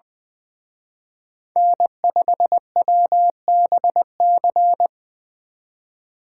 Morse code, heard as a keyed tone, about 24 wpm, 700 Hz. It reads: N5WBC
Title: Speed Racing - USA 1x3 Call Signs 20wpm